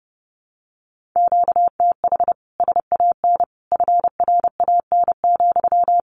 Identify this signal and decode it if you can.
Morse code, heard as a keyed tone, about 28 wpm, 700 Hz. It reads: QTH SAN FRAN,
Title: Common QSO Phrases Encore 30wpm